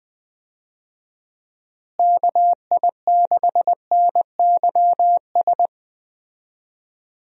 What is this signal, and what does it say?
Morse code, heard as a keyed tone, about 20 wpm, 700 Hz. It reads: KI6NYS
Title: USA Call Signs - 6 Characters - Repeated 3x - 20wpm